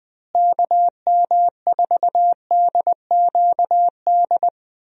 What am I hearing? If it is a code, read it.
Morse code, heard as a keyed tone, about 20 wpm, 700 Hz. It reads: KM4DQD
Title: USA Call Signs - 6 Characters - Repeated 3x - 20wpm